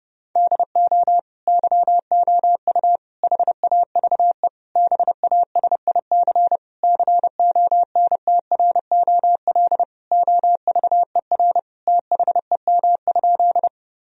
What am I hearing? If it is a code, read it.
Morse code, heard as a keyed tone, about 30 wpm, 700 Hz. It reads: DO YOU HAVE BASIC CONTROL OVER THEM?
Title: Sentences from Top 1000 Words - Review Part 3 - 30wpm